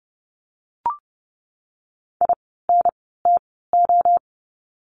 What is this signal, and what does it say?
Morse code, heard as a keyed tone, about 30 wpm, spaced as for 17 wpm, 700 Hz. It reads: INTO
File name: course / 30:10wpm / 230 Lesson 37f - Letter x - Words 30wpm